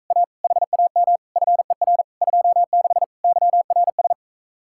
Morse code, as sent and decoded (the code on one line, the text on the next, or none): .- / .... .- -- / ..-. .-. / ..--- -.... / -.-- .-. ...
A HAM FER 26 YRS